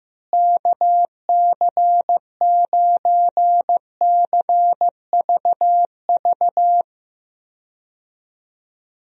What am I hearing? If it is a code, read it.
Morse code, heard as a keyed tone, about 15 wpm, 700 Hz. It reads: KC9CVV
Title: USA Call Signs - 6 Characters - Repeated 3x - 15wpm